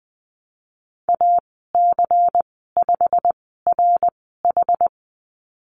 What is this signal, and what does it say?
Morse code, heard as a keyed tone, about 20 wpm, spaced as for 15 wpm, 700 Hz. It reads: AC5RH
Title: 154 Lesson 25g - Letter p - Call Signs 20wpm